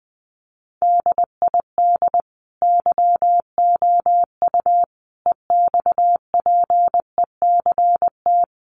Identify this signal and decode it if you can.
Morse code, heard as a keyed tone, about 20 wpm, 700 Hz. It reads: DID YOU EXPECT
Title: Sentences from Top 1100 Words 20wpm